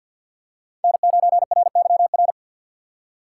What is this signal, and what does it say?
Morse code, heard as a keyed tone, about 52 wpm, 700 Hz. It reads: N9RYR
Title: USA Call Signs - 4x Character Spacing - 50wpm